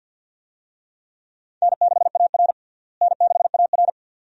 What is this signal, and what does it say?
Morse code, heard as a keyed tone, about 52 wpm, 700 Hz. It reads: N6AR N6AR
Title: USA Call Signs - 4 Characters - Repeated 2x - 50wpm